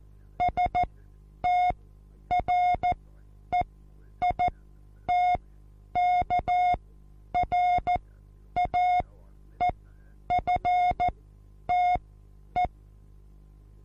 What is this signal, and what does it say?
Morse code, heard as a keyed tone, about 14 wpm, spaced as for 9 wpm, 720 Hz. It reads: STREITKRAEFTE
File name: Onlinelehrgang / morsen / DJ4UF Morsen23